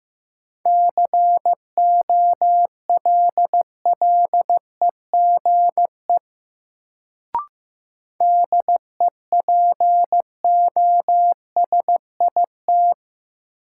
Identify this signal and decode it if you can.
Morse code, heard as a keyed tone, about 15 wpm, 700 Hz. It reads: COLLEGE DEPOSIT
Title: Sentences from Top 1300 Words - New Word Drill - 15wpm